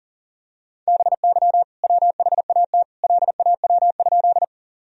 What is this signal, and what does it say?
Morse code, heard as a keyed tone, about 37 wpm, 700 Hz. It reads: BY WHAT LAW?
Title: Sentences from Top 1000 Words - Review Part 3 - 40wpm